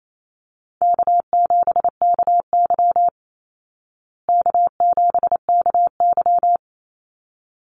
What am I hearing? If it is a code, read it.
Morse code, heard as a keyed tone, about 28 wpm, 700 Hz. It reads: K7KY K7KY
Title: Easy Call Signs Repeated 2x - 28wpm